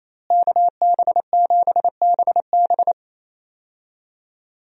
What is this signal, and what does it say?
Morse code, heard as a keyed tone, about 28 wpm, 700 Hz. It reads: KB7BB
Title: USA Call Signs - 5 Characters - Repeated 2x - 28wpm